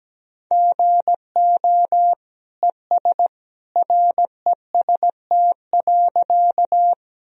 Morse code, heard as a keyed tone, about 17 wpm, 700 Hz. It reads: GO ES REST.
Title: Sentences from Top 500 Words and Abbreviations 17wpm